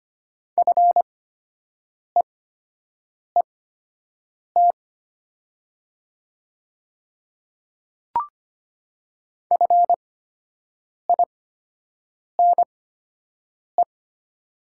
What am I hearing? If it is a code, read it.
Morse code, heard as a keyed tone, about 25 wpm, spaced as for 7 wpm, 690 Hz. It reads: FEET FINE
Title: 8x Character Spacing - Top 500 Words - 25wpm